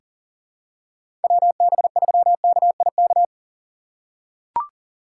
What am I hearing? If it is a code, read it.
Morse code, heard as a keyed tone, about 40 wpm, 700 Hz. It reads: WB3KIK
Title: USA Call Signs - 6 Characters - 6x Character Spacing - 40wpm